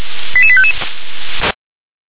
Five-tone selective call (ZVEI), heard as quick runs of five tones, tones of about 70 ms. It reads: 88855